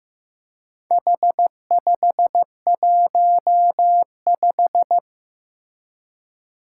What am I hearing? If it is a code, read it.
Morse code, heard as a keyed tone, about 15 wpm, 700 Hz. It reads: H515
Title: Four Challenging Letter-Number Combinations 15wpm